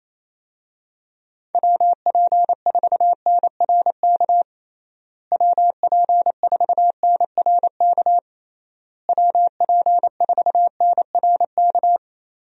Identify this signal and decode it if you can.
Morse code, heard as a keyed tone, about 28 wpm, 700 Hz. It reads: WP4NRK WP4NRK WP4NRK
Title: USA Call Signs - 6 Characters - Repeated 3x - 28wpm